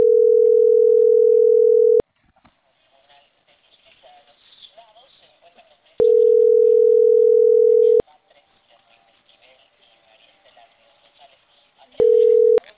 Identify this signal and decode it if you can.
Ringback tone (North America), 440 + 480 Hz.